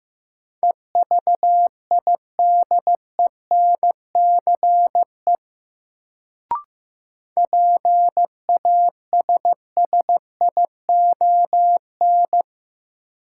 Morse code, heard as a keyed tone, about 15 wpm, 700 Hz. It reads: EVIDENCE PASSION
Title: Sentences from Top 1300 Words - New Word Drill Alt - 15wpm